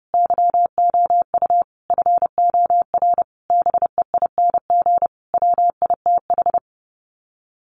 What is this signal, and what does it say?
Morse code, heard as a keyed tone, about 30 wpm, 700 Hz. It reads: YOU FOR BEING WITH
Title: Sets of 5 Words 30wpm